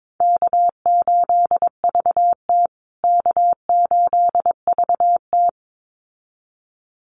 Morse code, heard as a keyed tone, about 22 wpm, 700 Hz. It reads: K8VT K8VT
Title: USA Call Signs Repeated 2x 22wpm